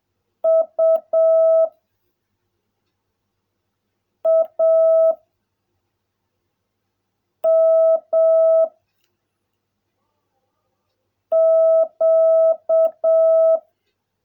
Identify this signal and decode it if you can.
Morse code, heard as a keyed tone, about 7 wpm, 660 Hz. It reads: U A M Q
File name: jh8cbh / 2021cw / cw16a